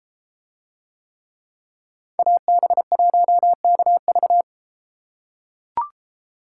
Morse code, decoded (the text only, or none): AB1KV